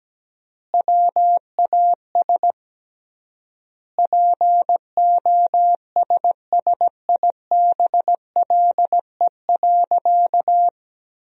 Morse code, heard as a keyed tone, about 17 wpm, 700 Hz. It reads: WAS POSSIBLE.
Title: 3x Word-Spacing - Sentences from Top 500 Words - 17wpm